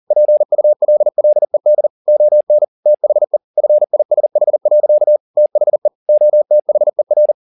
Morse code, heard as a keyed tone, about 40 wpm, 600 Hz. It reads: PULLED ON THE FISH. THE OTHER
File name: the old man and the sea 2 40wpm